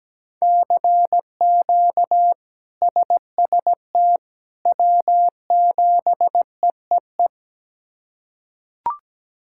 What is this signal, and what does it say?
Morse code, heard as a keyed tone, about 17 wpm, 700 Hz. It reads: CQ SST W7EEE
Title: K1USN SST QSO Practice - 17wpm